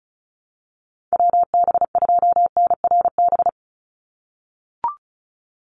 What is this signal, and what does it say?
Morse code, heard as a keyed tone, about 34 wpm, 700 Hz. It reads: WB2NRB
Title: USA Call Signs 35wpm